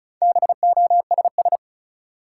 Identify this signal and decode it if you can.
Morse code, heard as a keyed tone, about 36 wpm, 700 Hz. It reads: BOSS
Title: Sentences from Top 700 Words - New Word Drill - 35wpm